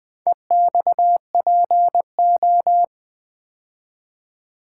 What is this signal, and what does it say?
Morse code, heard as a keyed tone, about 20 wpm, 700 Hz. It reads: EXPO
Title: Build A Word - 2 Letters At A Time - 6-Letter Words - 20wpm